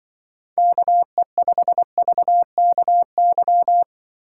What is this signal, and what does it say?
Morse code, heard as a keyed tone, about 24 wpm, 700 Hz. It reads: KE5VKY
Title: Speed Racing - USA 2x3 Call Signs 20wpm